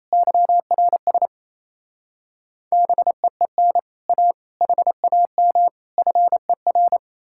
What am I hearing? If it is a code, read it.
Morse code, heard as a keyed tone, about 38 wpm, 700 Hz. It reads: YRS BEEN A HAM FER